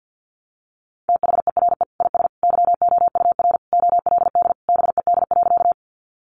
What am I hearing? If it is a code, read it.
Morse code, heard as a keyed tone, about 50 wpm, 700 Hz. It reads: THERE IS YOUR OLD BED.
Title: Sentences from Top 500 Words 50wpm